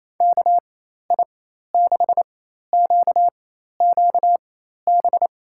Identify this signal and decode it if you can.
Morse code, heard as a keyed tone, about 27 wpm, spaced as for 13 wpm, 700 Hz. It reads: KI6QQB
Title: USA Call Signs - 6 Characters - 4x Character Spacing - 28wpm